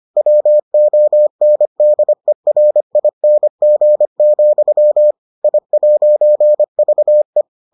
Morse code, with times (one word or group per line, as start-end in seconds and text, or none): 0.2-5.1 WONDERING,
5.4-7.4 I'VE